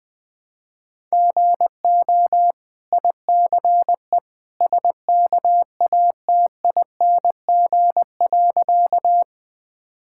Morse code, heard as a keyed tone, about 20 wpm, 700 Hz. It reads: GO ICE SKATING.
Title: Easy Sentences with Hints - 20wpm